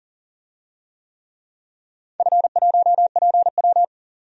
Morse code, heard as a keyed tone, about 41 wpm, 700 Hz. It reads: F1PW
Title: DXCC-100 Call Signs - 4 Characters - 40wpm